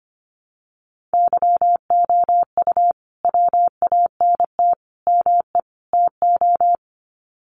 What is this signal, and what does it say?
Morse code, heard as a keyed tone, about 25 wpm, 700 Hz. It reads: YOU WANT ME TO
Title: Sets of 4 Words 25wpm